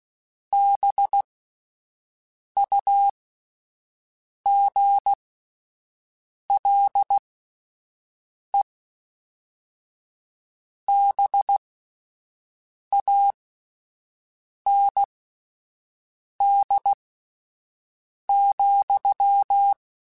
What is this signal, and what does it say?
Morse code, heard as a keyed tone, about 16 wpm, spaced as for 5 wpm, 800 Hz. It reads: BUGLE BAND,